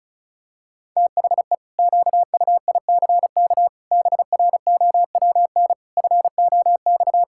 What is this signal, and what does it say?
Morse code, heard as a keyed tone, about 34 wpm, 700 Hz. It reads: THE QUICK BROWN FOX
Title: Pangram Challenge v2 - 35wpm